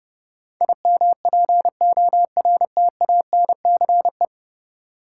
Morse code, heard as a keyed tone, about 30 wpm, 700 Hz. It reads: IMPORTANCE
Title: Sentences from Top 1200 Words - New Word Drill - 30wpm